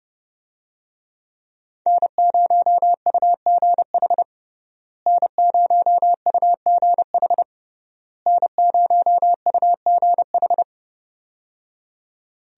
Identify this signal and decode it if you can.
Morse code, heard as a keyed tone, about 30 wpm, 700 Hz. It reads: N0UGH N0UGH N0UGH
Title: USA Call Signs - 5 Characters - Repeated 3x - 30wpm